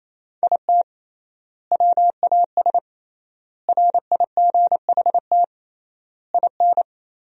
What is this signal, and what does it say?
Morse code, heard as a keyed tone, about 28 wpm, 700 Hz. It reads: IT WAS RIGHT IN
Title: 3x Word-Spacing - Sentences from Top 500 Words - 28wpm